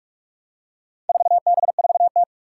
Morse code, heard as a keyed tone, about 44 wpm, 700 Hz. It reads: 4B4T
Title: Four Challenging Letter-Number Combinations 45wpm